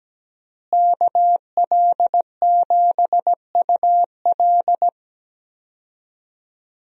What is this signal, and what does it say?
Morse code, heard as a keyed tone, about 17 wpm, 700 Hz. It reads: KL7UL
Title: USA Call Signs - 5 Characters - Repeated 3x - 17wpm